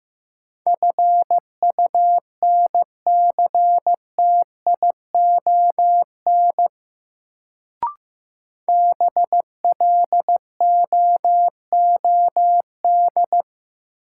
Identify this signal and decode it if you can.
Morse code, heard as a keyed tone, about 15 wpm, 700 Hz. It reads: FUNCTION BLOOD
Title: Sentences from Top 1100 Words - New Word Drill Alt - 15wpm